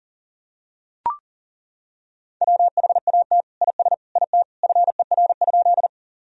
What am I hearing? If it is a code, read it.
Morse code, heard as a keyed tone, about 40 wpm, 700 Hz. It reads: WHAT IS IT FER?
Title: Sentences from Top 100 Words and Abbreviations 40wpm